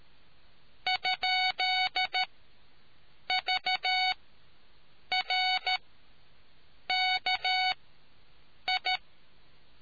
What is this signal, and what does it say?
Morse code, heard as a keyed tone, about 13 wpm, spaced as for 6 wpm, 740 Hz. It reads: ?VRKI